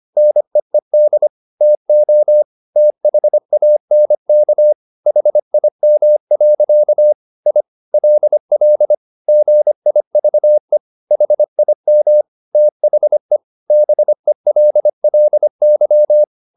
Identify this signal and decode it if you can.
Morse code, heard as a keyed tone, about 25 wpm, 600 Hz. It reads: NEED TO THANK HIM. I LL GIVE HIM THE BELLY
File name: the old man and the sea part 1 25wpm